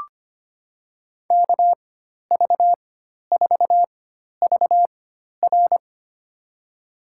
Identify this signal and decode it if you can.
Morse code, heard as a keyed tone, about 25 wpm, spaced as for 12 wpm, 700 Hz. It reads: KV4VR
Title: USA Call Signs - 5 Characters - 4x Character Spacing - 25wpm